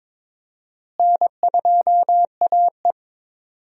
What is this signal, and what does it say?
Morse code, heard as a keyed tone, about 22 wpm, 700 Hz. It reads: N2AE
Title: USA Call Signs - 4 Characters - Repeated 3x - 22wpm